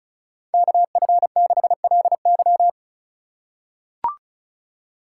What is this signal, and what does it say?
Morse code, heard as a keyed tone, about 34 wpm, 700 Hz. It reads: KF6LY
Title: USA Call Signs - 5 Characters - 4x Character Spacing - 35wpm